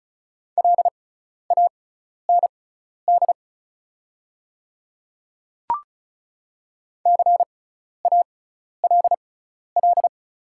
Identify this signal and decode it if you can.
Morse code, heard as a keyed tone, about 34 wpm, spaced as for 12 wpm, 690 Hz. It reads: LAND CALL
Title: 6x Character Spacing - Top 500 Words - 35wpm